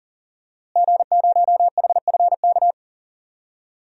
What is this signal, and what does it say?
Morse code, heard as a keyed tone, about 40 wpm, 700 Hz. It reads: G0HFK
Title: DXCC-100 Call Signs - 5 Characters - Repeated 3x - 40wpm